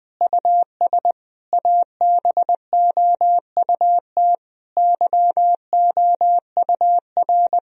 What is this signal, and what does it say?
Morse code, heard as a keyed tone, about 20 wpm, 700 Hz. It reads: US ABOUT YOUR
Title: Sentences from Top 1000 Words - Review Part 3 - 20wpm